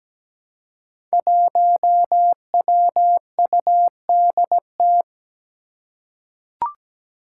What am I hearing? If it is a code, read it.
Morse code, heard as a keyed tone, about 17 wpm, 700 Hz. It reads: 1WUDT
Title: Five Challenging Letter-Number Combinations 17wpm